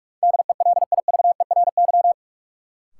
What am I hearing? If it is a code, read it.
Morse code, heard as a keyed tone, about 44 wpm, 700 Hz. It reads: DELIVERY
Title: Sentences from Top 1500 Words - New Word Drill - 45wpm